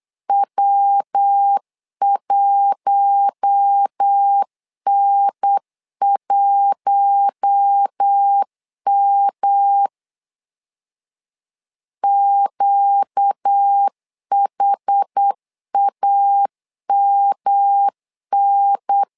Morse code, heard as a keyed tone, about 8 wpm, 800 Hz. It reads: W1N1M QHAMN
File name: LEZIONI / 4 Q1 8wpm 3035